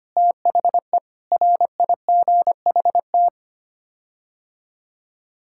Morse code, heard as a keyed tone, about 25 wpm, 700 Hz. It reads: THE RIGHT
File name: Sets of 9 Words - 25wpm